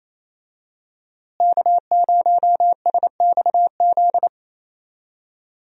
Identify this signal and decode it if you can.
Morse code, heard as a keyed tone, about 28 wpm, 700 Hz. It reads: K0SXZ
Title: USA Call Signs - 5 Characters - Repeated 3x - 28wpm